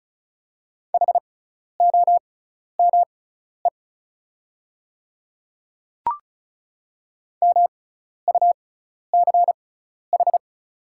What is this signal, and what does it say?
Morse code, heard as a keyed tone, about 34 wpm, spaced as for 12 wpm, 700 Hz. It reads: HOME MUCH
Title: 6x Character Spacing - Top 200 Words - 35wpm